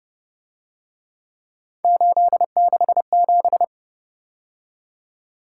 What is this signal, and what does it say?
Morse code, heard as a keyed tone, about 29 wpm, 700 Hz. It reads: 867
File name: Speed-Racing - Three Letter-Number Combinations - 25wpm